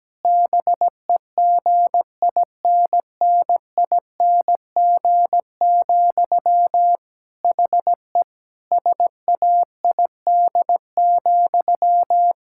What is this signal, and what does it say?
Morse code, heard as a keyed tone, about 17 wpm, 700 Hz. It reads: BEGINNING, HE SAID,
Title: Sentences from Top 1600 Words 17wpm